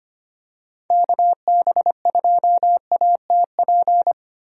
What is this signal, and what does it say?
Morse code, heard as a keyed tone, about 25 wpm, 700 Hz. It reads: KB2ATP
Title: USA Call Signs - 6 Characters - 25wpm